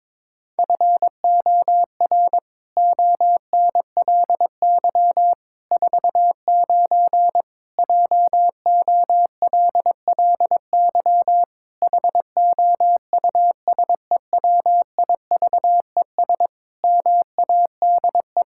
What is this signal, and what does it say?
Morse code, heard as a keyed tone, about 22 wpm, 700 Hz. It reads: FOR ONLY 49 JOLLY HOUSEWIVES MADE
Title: Pangram Challenge v2 - 22wpm